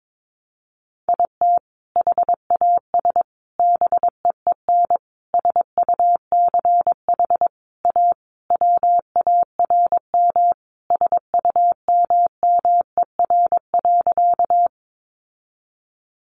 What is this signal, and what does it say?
Morse code, heard as a keyed tone, about 22 wpm, 700 Hz. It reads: IT HAS BEEN SUCH A WARM SUMMER.
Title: Sentences from Top 1100 Words 22wpm